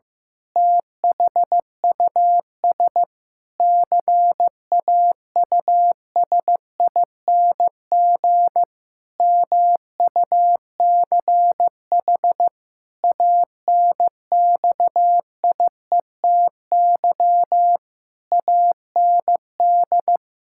Morse code, heard as a keyed tone, about 15 wpm, 700 Hz. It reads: THUS CAUSING MUCH ANXIETY AND
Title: Pangram Challenge v2 - 15wpm